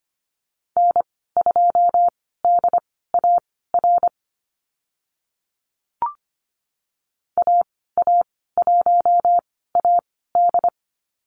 Morse code, heard as a keyed tone, about 25 wpm, spaced as for 16 wpm, 700 Hz. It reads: N2DAR AA1AD
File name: Lesson 15g - Number 5 - Call Signs 25wpm